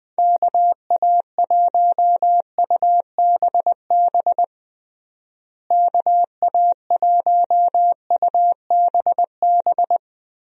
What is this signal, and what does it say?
Morse code, heard as a keyed tone, about 20 wpm, 700 Hz. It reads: KA1UBB KA1UBB
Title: USA Call Signs - 6 Characters - Repeated 2x - 20wpm